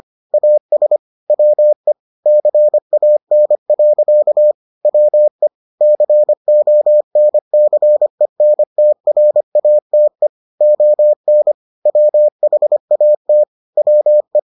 Morse code, heard as a keyed tone, about 25 wpm, 600 Hz. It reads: AS WE CAN. WE CONCENTRATE ON WHAT WE